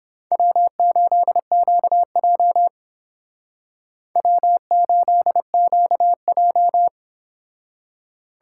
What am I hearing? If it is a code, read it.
Morse code, heard as a keyed tone, about 28 wpm, 700 Hz. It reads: W8QJ W8QJ